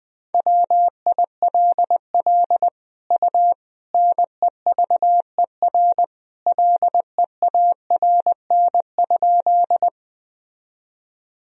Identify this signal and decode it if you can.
Morse code, heard as a keyed tone, about 20 wpm, 700 Hz. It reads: WILL U NEVER LEARN?
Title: Sentences from Top 300 Words and Abbreviations 20wpm